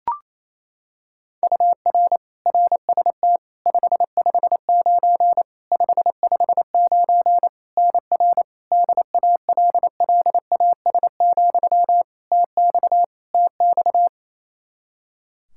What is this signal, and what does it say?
Morse code, heard as a keyed tone, about 28 wpm, 700 Hz. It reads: UR RST 559 559 NR DALLAS, TX TX
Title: Practice QSO - Exchange 1 - Beginning - 28wpm